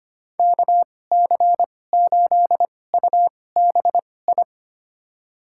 Morse code, heard as a keyed tone, about 25 wpm, spaced as for 18 wpm, 700 Hz. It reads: KC8UBI
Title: USA Call Signs - 2x Character Spacing - 25wpm